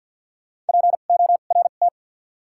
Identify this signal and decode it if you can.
Morse code, heard as a keyed tone, about 52 wpm, spaced as for 36 wpm, 700 Hz. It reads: PORT
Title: 2x Character Spacing - Top 500 Words - 50wpm